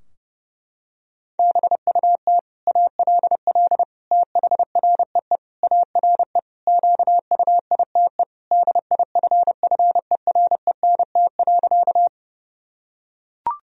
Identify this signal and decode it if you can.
Morse code, heard as a keyed tone, about 31 wpm, 700 Hz. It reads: BUT ALL THREE ARE QUITE DIFFERENT.
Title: Sentences from Top 1000 Words - Review Part 3 - 30wpm